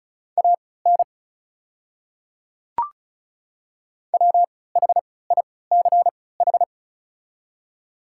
Morse code, heard as a keyed tone, about 36 wpm, spaced as for 20 wpm, 700 Hz. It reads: AN WHICH
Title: 3x Character Spacing - Top 100 Words - 35wpm